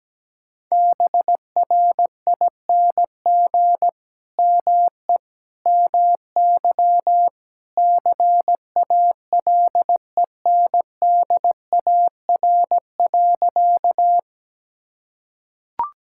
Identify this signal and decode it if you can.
Morse code, heard as a keyed tone, about 17 wpm, 700 Hz. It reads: BRING ME MY CALENDAR.
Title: Sentences from Top 1600 Words 17wpm